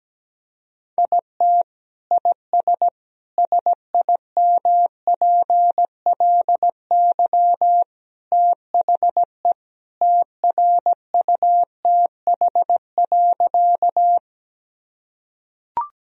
Morse code, heard as a keyed tone, about 17 wpm, 700 Hz. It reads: IT IS SIMPLY THE TRUTH.
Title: Sentences from Top 1000 Words - Review Part 4 - 17wpm